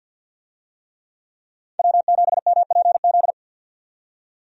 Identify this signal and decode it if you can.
Morse code, heard as a keyed tone, about 50 wpm, 700 Hz. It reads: W7MPZ
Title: USA Call Signs - 5 Characters - 50wpm